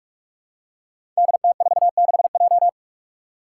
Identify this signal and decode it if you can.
Morse code, heard as a keyed tone, about 44 wpm, 700 Hz. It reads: DT46J
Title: Five Challenging Letter-Number Combinations 45wpm